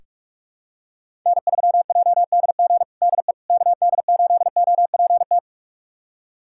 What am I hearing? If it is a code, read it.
Morse code, heard as a keyed tone, about 42 wpm, 700 Hz. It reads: N3JDG DE KD8OPT
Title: USA Callsign DE Callsign 45wpm